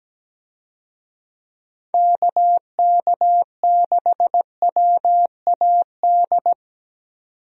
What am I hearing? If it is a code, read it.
Morse code, heard as a keyed tone, about 17 wpm, 700 Hz. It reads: KK6WAD
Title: USA Call Signs - 6 Characters - 17wpm